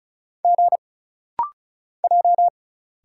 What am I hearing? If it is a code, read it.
Morse code, heard as a keyed tone, about 36 wpm, 700 Hz. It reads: G J
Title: Single Letters - Rapid-Fire 35wpm